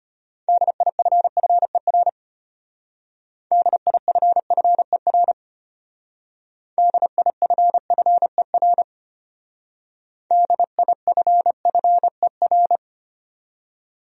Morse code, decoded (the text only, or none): DIFFER DIFFER DIFFER DIFFER